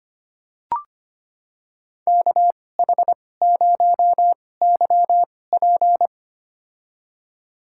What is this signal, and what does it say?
Morse code, heard as a keyed tone, about 25 wpm, spaced as for 18 wpm, 700 Hz. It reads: KH0YP
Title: USA Call Signs - 5 Characters - 2x Character Spacing - 25wpm